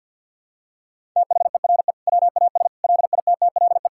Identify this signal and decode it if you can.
Morse code, heard as a keyed tone, about 48 wpm, 700 Hz. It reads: THERE WAS LITTLE
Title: Sentences from Top 1000 Words - Review Part 6 - 50wpm